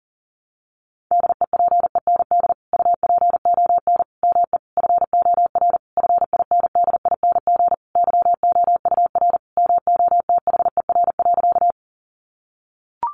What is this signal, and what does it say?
Morse code, heard as a keyed tone, about 40 wpm, 700 Hz. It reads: DEPEND UPON ME FOR FINDING YOUR MOTHER.
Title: Sentences from Top 1300 Words 40wpm